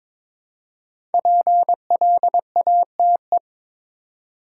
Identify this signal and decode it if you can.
Morse code, heard as a keyed tone, about 22 wpm, 700 Hz. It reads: PLATE
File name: 5 Letter Words - First 3 Letters Given - 22wpm